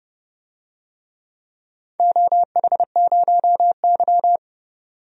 Morse code, heard as a keyed tone, about 30 wpm, 700 Hz. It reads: OH0Y
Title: DXCC-100 Call Signs - 4 Characters - 30wpm